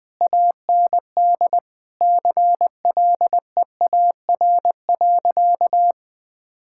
Morse code, heard as a keyed tone, about 21 wpm, 700 Hz. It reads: AND CLEAR.
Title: Sentences from Top 1000 Words - Review Part 6 - 20wpm